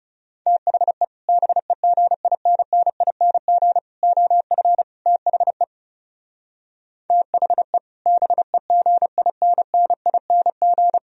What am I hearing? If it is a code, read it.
Morse code, heard as a keyed tone, about 38 wpm, 690 Hz. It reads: THE BEGINNING OF THE THE BEGINNING